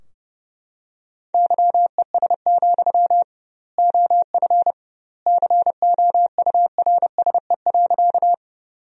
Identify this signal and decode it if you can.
Morse code, heard as a keyed tone, about 30 wpm, 700 Hz. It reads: YES, OF COURSE.
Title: 2x Word-Spacing - Sentences from Top 500 Words - 30wpm